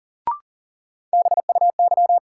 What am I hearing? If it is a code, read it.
Morse code, heard as a keyed tone, about 40 wpm, 700 Hz. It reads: BUY
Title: Sentences from Top 700 Words - New Word Drill - 40wpm